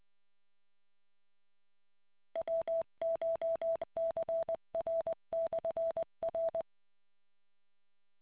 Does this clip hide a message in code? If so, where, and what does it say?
Yes: Morse code from 2.4 to 6.6 s, heard as a keyed tone, about 20 wpm, 660 Hz. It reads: WCHCR/R